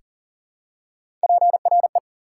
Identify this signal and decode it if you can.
Morse code, heard as a keyed tone, about 40 wpm, 700 Hz. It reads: PRE
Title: Prefixes and Suffixes 40wpm